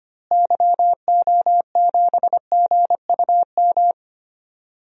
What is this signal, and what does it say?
Morse code, heard as a keyed tone, about 25 wpm, 700 Hz. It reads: YO7GUM
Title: DXCC-100 Call Signs - 6 Characters - Repeated 2x - 25wpm